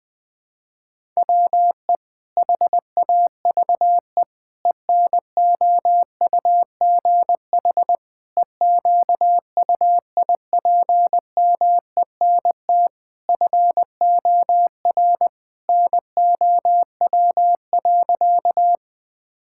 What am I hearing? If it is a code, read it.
Morse code, heard as a keyed tone, about 21 wpm, 700 Hz. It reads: WE HAVE ENOUGH EQUIPMENT FOR NOW.
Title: Sentences from Top 1000 Words - Review Part 5 - 20wpm